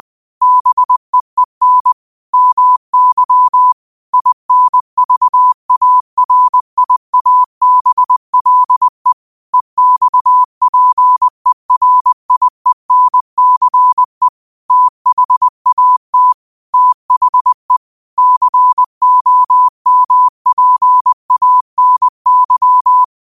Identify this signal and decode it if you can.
Morse code, heard as a keyed tone, about 20 wpm, 1000 Hz. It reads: BEEN MY INVARIABLE EXPERIENCE THAT THE COMPANY
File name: qotdfiles / 20wpm